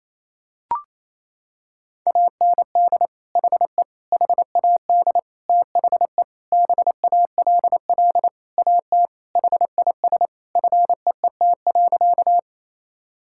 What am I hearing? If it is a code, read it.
Morse code, heard as a keyed tone, about 27 wpm, 700 Hz. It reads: AND HE HAD THE BALL AT HIS FEET.
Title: Sentences from Top 1000 Words - Review Part 6 - 28wpm